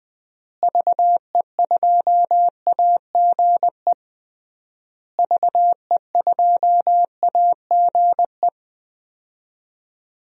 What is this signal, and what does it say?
Morse code, heard as a keyed tone, about 20 wpm, 700 Hz. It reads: VE2AGE VE2AGE
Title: DXCC-100 Call Signs - 6 Characters - Repeated 3x - 20wpm